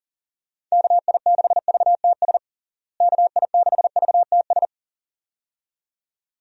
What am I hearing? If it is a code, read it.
Morse code, heard as a keyed tone, about 40 wpm, 700 Hz. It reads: KI6VTS KI6VTS
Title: USA Call Signs - 6 Characters - Repeated 2x - 40wpm